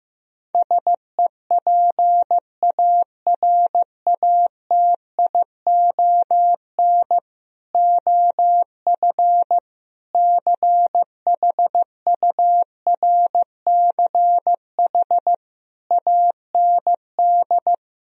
Morse code, heard as a keyed tone, about 15 wpm, 700 Hz. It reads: SEPARATION OF CHURCH AND
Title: Speed-Racing - Sets of 5 Words 15wpm